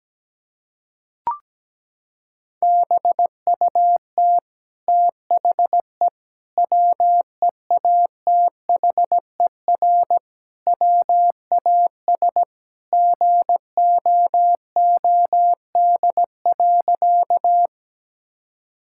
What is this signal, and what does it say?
Morse code, heard as a keyed tone, about 17 wpm, 700 Hz. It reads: BUT THE WEATHER WAS GOOD.
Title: Sentences from Top 1000 Words - Review Part 5 - 17wpm